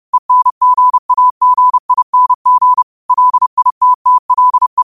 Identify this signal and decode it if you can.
Morse code, heard as a keyed tone, about 30 wpm, 1000 Hz. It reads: ENGAGING LITTLE
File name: qotdfiles / 30wpm